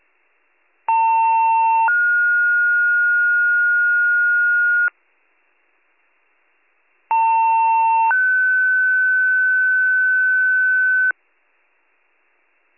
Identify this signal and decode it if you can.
Two-tone page, heard as a one-second tone, then a three-second tone, about 900 then 1430 Hz, about 900 then 1500 Hz. It is heard from 0.9 to 11.1 s.